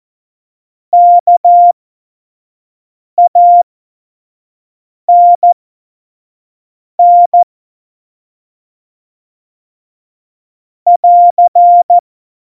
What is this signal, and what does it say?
Morse code, heard as a keyed tone, about 14 wpm, spaced as for 5 wpm, 700 Hz. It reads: KANN <AR>
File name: Morsekurs / Witze / Witz 04. Dezember 2018, 25 F70